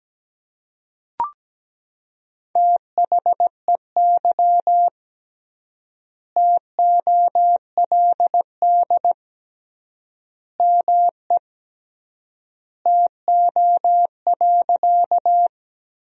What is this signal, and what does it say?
Morse code, heard as a keyed tone, about 17 wpm, 700 Hz. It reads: THEY TOLD ME TO.
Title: 3x Word-Spacing - Sentences from Top 400 Words - 17wpm